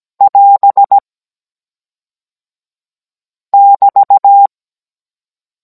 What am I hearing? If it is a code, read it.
Morse code, heard as a keyed tone, about 17 wpm, 800 Hz. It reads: <AS> <BT>
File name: Cours-NR19-16mm